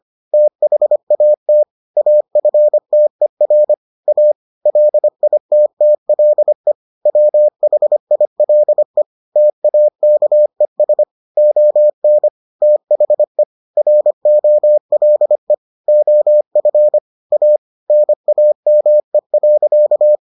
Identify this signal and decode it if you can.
Morse code, heard as a keyed tone, about 25 wpm, 600 Hz. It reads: THAT AFTER A LITTLE WHILE TAKES ON THE ROLE OF A NAME.